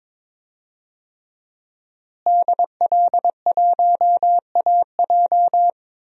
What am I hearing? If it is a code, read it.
Morse code, heard as a keyed tone, about 22 wpm, 700 Hz. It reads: DL1AJ
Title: DXCC-100 Call Signs - 5 Characters - Repeated 3x - 22wpm